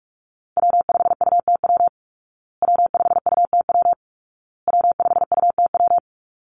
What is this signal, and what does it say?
Morse code, heard as a keyed tone, about 44 wpm, 700 Hz. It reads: W5UTW W5UTW W5UTW
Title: Call Signs Repeated 2x - 45wpm